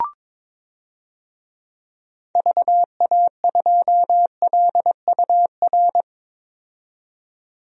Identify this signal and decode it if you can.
Morse code, heard as a keyed tone, about 22 wpm, 700 Hz. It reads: VA2LUR
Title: DXCC-100 Call Signs - 6 Characters - 22wpm